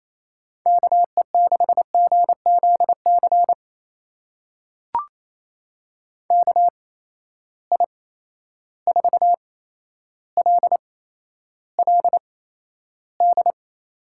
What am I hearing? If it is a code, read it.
Morse code, heard as a keyed tone, about 28 wpm, spaced as for 12 wpm, 700 Hz. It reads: KE6GZC KI4LLD
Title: USA Call Signs - 6 Characters - 8x Character Spacing - 28wpm